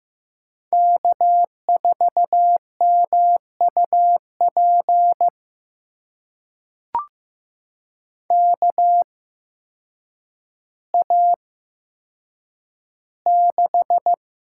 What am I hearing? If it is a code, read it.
Morse code, heard as a keyed tone, about 15 wpm, spaced as for 8 wpm, 700 Hz. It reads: K4MUP KA6
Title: USA Call Signs - 8x Character Spacing - 15wpm